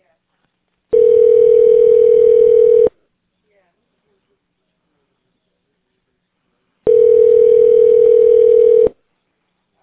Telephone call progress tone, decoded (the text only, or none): ringback tone